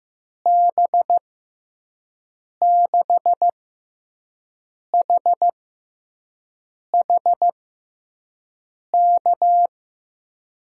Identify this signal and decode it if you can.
Morse code, heard as a keyed tone, about 15 wpm, spaced as for 5 wpm, 700 Hz. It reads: B6HH K
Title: USA Call Signs - 6 Characters - 6x Character Spacing - 15wpm